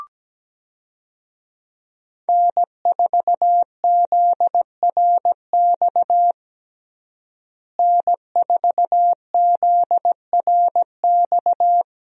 Morse code, heard as a keyed tone, about 17 wpm, 700 Hz. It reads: N4ZRX N4ZRX
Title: USA Call Signs - 5 Characters - Repeated 3x - 17wpm